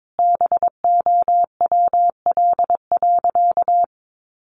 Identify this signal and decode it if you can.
Morse code, heard as a keyed tone, about 22 wpm, 700 Hz. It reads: BOWL.
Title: Sentences from Top 1200 Words 22wpm